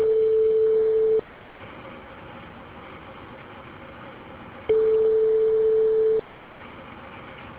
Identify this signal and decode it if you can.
Ringback tone (France), 440 Hz.